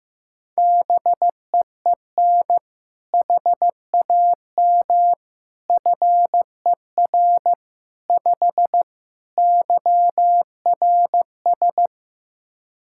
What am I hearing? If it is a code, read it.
Morse code, heard as a keyed tone, about 15 wpm, 700 Hz. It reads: BEEN HAM FER 5 YRS
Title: K1USN SST QSO Practice 15wpm